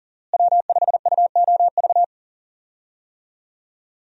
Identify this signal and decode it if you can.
Morse code, heard as a keyed tone, about 39 wpm, 700 Hz. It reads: W5UOV
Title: USA Call Signs - 5 Characters - Repeated 2x - 40wpm